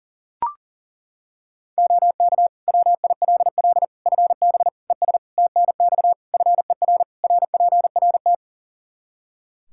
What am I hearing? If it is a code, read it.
Morse code, heard as a keyed tone, about 37 wpm, 700 Hz. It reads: OK WILL FB ES TNX FER RPRT
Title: Practice QSO - Exchange 2 - Info - 40wpm